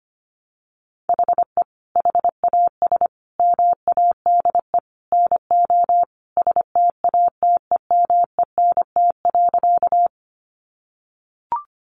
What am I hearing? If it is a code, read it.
Morse code, heard as a keyed tone, about 25 wpm, 700 Hz. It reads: HE HAS MADE NO STATEMENT.
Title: Sentences from Top 1200 Words 25wpm